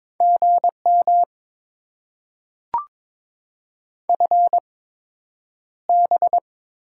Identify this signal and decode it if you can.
Morse code, heard as a keyed tone, about 22 wpm, spaced as for 9 wpm, 700 Hz. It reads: GM FB
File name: Top 20 QSO Elements - 8x Character Spacing - 22wpm